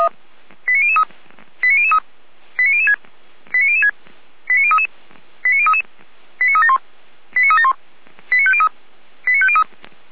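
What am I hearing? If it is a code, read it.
Five-tone selective call (ZVEI), heard as quick runs of five tones, tones of about 70 ms. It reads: 89003 89003 89006 89006 89033 89033 89371 89371 89503 89503
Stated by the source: ZVEI five-tone selective call received on an SDR, from the Feuerwehr Orlamünde (fire brigade) audio server.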